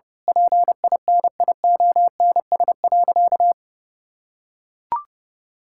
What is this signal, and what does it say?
Morse code, heard as a keyed tone, about 30 wpm, 700 Hz. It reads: PINIONS.
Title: Pangram Challenge v2 - 30wpm